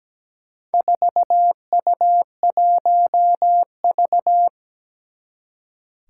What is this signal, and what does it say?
Morse code, heard as a keyed tone, about 17 wpm, 700 Hz. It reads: 4U1V
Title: Four Challenging Letter-Number Combinations 17wpm